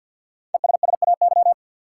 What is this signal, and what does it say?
Morse code, heard as a keyed tone, about 47 wpm, 700 Hz. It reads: ESSAY
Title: Five Letter Words - 50wpm